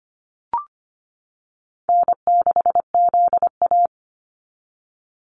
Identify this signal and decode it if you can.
Morse code, heard as a keyed tone, about 25 wpm, 690 Hz. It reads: N6ZA
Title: NA SOTA Activator Callsigns 25wpm